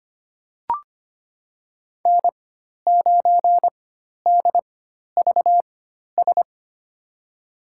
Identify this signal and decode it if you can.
Morse code, heard as a keyed tone, about 25 wpm, spaced as for 12 wpm, 700 Hz. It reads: N9DVS
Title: USA Call Signs - 5 Characters - 4x Character Spacing - 25wpm